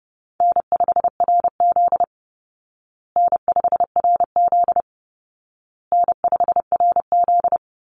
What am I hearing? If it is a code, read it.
Morse code, heard as a keyed tone, about 30 wpm, 700 Hz. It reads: N5RZ N5RZ N5RZ
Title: Easy Call Signs Repeated 3x - 30wpm